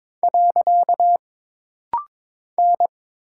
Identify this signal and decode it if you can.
Morse code, heard as a keyed tone, about 22 wpm, 700 Hz. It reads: . N
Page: This rapid-fire timing is geared to push your ICR (Instant Character Recognition) skills.